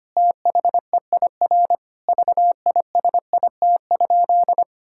Morse code, heard as a keyed tone, about 25 wpm, 700 Hz. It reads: THEIR VISIT?
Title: Sentences from Top 1600 Words 25wpm